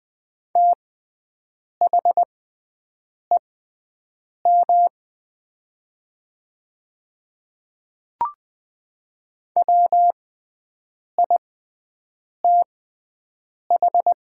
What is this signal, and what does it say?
Morse code, heard as a keyed tone, about 20 wpm, spaced as for 7 wpm, 700 Hz. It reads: THEM WITH
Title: 6x Character Spacing - Top 100 Words - 20wpm